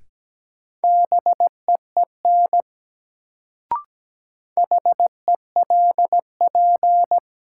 Morse code, heard as a keyed tone, about 17 wpm, 700 Hz. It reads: BEEN HELP
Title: Sets of 2 Words - New Word Drill - 17wpm